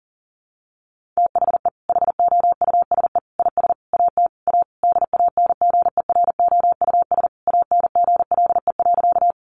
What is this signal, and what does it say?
Morse code, heard as a keyed tone, about 40 wpm, 700 Hz. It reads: THE HOUSE IS AT A DANGEROUS ANGLE.
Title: Sentences from Top 1500 Words 40wpm